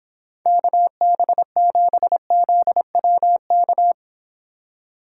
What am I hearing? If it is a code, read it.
Morse code, heard as a keyed tone, about 26 wpm, 700 Hz. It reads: KB7ZWK